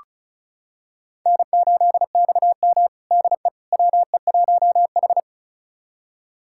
Morse code, heard as a keyed tone, about 36 wpm, 700 Hz. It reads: N8XM DE WE1H
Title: USA Callsign DE Callsign - 4 Characters - 35wpm